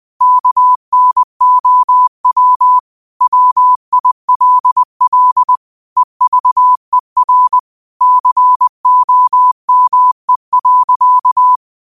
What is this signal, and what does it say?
Morse code, heard as a keyed tone, about 20 wpm, 1000 Hz. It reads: KNOW WILL EVER COME.